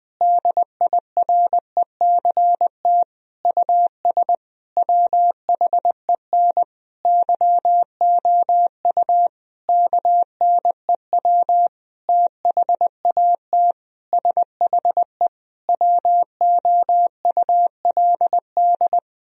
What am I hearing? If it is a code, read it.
Morse code, heard as a keyed tone, about 20 wpm, 700 Hz. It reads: DIRECT US WHEN YOU KNEW THAT SHE WOULD
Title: Sentences from Top 1000 Words - Review Part 2 - 20wpm